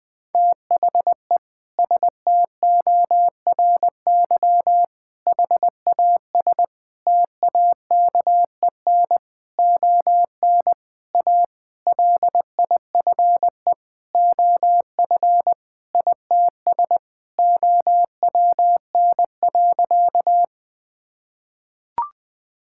Morse code, heard as a keyed tone, about 21 wpm, 700 Hz. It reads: THE STORY HAS TAKEN ON A LIFE OF ITS OWN.
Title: Sentences with 8-Word Phrases - Spoken Beginning - 20wpm